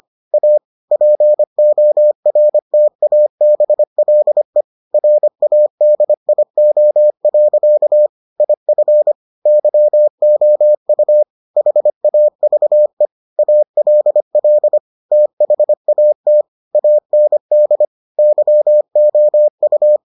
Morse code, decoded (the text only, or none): A PORTABLE RADIO. IF YOU HAVE ALL THAT AND YOU